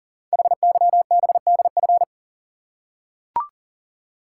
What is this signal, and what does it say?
Morse code, heard as a keyed tone, about 38 wpm, 690 Hz. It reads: HYBDF
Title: Code Groups - 5 Letter:Numbers - 40wpm